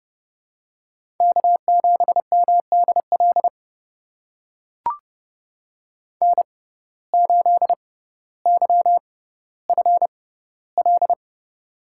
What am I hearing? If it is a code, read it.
Morse code, heard as a keyed tone, about 30 wpm, spaced as for 15 wpm, 700 Hz. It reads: K7MDL N8YFL
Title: USA Call Signs - 5 Characters - 6x Character Spacing - 30wpm